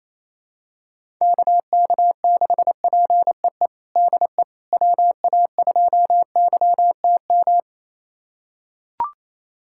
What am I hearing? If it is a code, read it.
Morse code, heard as a keyed tone, about 28 wpm, 700 Hz. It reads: KK6PEE DE WA2YTM
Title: USA Callsign DE Callsign - 6 Characters - 28wpm